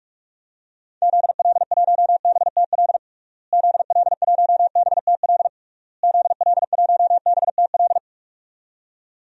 Morse code, heard as a keyed tone, about 42 wpm, 700 Hz. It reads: ZL1BTL ZL1BTL ZL1BTL
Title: DXCC-100 Call Signs - 6 Characters - Repeated 3x - 45wpm